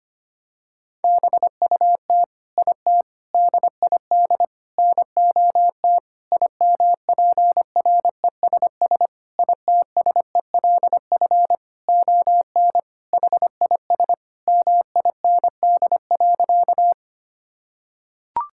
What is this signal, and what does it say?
Morse code, heard as a keyed tone, about 25 wpm, 700 Hz. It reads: BUT IT DID NOT IMPRESS ITSELF ON HIS MIND.
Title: Sentences from Top 1600 Words 25wpm